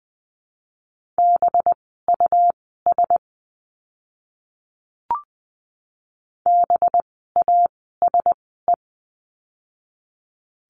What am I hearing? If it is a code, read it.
Morse code, heard as a keyed tone, about 20 wpm, spaced as for 14 wpm, 700 Hz. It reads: BUS BASE
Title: 185 Lesson 30f - Letter b - Words 20wpm